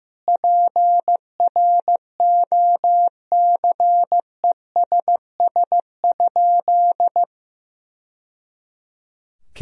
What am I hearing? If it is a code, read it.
Morse code, heard as a keyed tone, about 16 wpm, 700 Hz. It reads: PROCESS?
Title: Sentences from Top 1000 Words - Review Part 3 - 15wpm